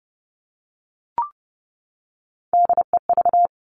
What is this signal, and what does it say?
Morse code, heard as a keyed tone, about 30 wpm, 700 Hz. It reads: DEV
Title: Three Challenging Letter-Number Combinations 30wpm